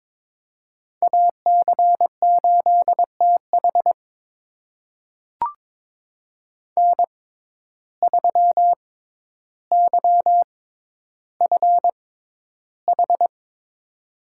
Morse code, heard as a keyed tone, about 22 wpm, spaced as for 11 wpm, 700 Hz. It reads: AC8TH N3YFH
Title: USA Call Signs - 6x Character Spacing - 22wpm